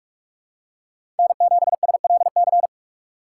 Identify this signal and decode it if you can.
Morse code, heard as a keyed tone, about 47 wpm, 700 Hz. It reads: N7SLC
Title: USA Call Signs - 5 Characters - 6x Character Spacing - 45wpm